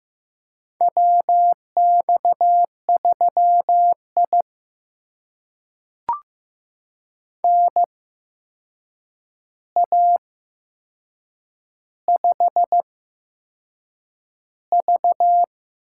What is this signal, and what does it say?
Morse code, heard as a keyed tone, about 15 wpm, spaced as for 6 wpm, 700 Hz. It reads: WX3I NA5V
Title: USA Call Signs - 4 Characters - 8x Character Spacing - 15wpm